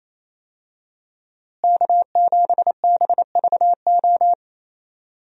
Morse code, heard as a keyed tone, about 28 wpm, 700 Hz. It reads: K7BVO
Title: USA Call Signs - 5 Characters - Repeated 2x - 28wpm